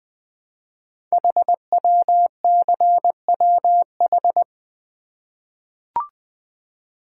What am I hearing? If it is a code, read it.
Morse code, heard as a keyed tone, about 20 wpm, 700 Hz. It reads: HWCWH
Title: Code Groups - 5 Letters - 20wpm